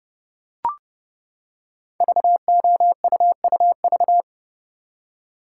Morse code, heard as a keyed tone, about 30 wpm, 700 Hz. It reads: VOUUV
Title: Code Groups - 5 Letters - 30wpm